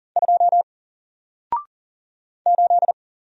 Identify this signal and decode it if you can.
Morse code, heard as a keyed tone, about 40 wpm, 700 Hz. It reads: 2 8
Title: Single Character (Letter:Number:Punctuation:Prosign) 40wpm